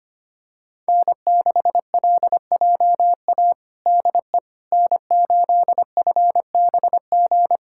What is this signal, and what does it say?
Morse code, heard as a keyed tone, about 25 wpm, 700 Hz. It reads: N6LJA DE N8FBG
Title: USA Callsign DE Callsign - 5 Characters - 25wpm